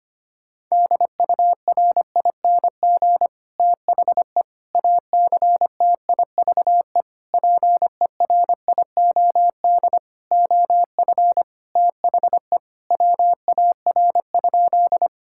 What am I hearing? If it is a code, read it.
Morse code, heard as a keyed tone, about 25 wpm, 700 Hz. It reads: DURING THE ACTIVE PERIOD OF THE WAR?
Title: Sentences from Top 1000 Words - Review Part 6 - 25wpm